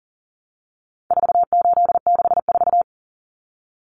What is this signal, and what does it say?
Morse code, heard as a keyed tone, about 40 wpm, 700 Hz. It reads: 4864
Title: Numbers - 4-Digit 40wpm